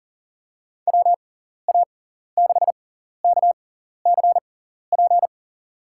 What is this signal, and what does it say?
Morse code, heard as a keyed tone, about 37 wpm, spaced as for 14 wpm, 700 Hz. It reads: WA6KCP
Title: USA Call Signs - 6x Character Spacing - 40wpm